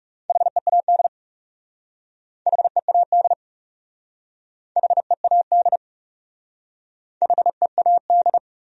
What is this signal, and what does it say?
Morse code, heard as a keyed tone, about 40 wpm, 700 Hz. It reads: HEAD HEAD HEAD HEAD